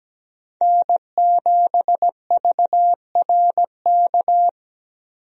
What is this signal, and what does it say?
Morse code, heard as a keyed tone, about 17 wpm, 700 Hz. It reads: N7VRK
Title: USA Call Signs - 5 Characters - Repeated 2x - 17wpm